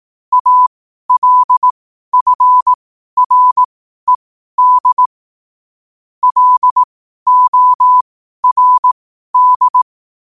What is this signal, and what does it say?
Morse code, heard as a keyed tone, about 18 wpm, spaced as for 12 wpm, 1000 Hz. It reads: ALFRED LORD
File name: qotdfiles / 12wpm